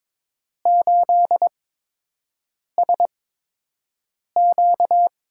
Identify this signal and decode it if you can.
Morse code, heard as a keyed tone, about 22 wpm, spaced as for 6 wpm, 700 Hz. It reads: 8SQ
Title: USA Call Signs - 5 Characters - 8x Character Spacing - 22wpm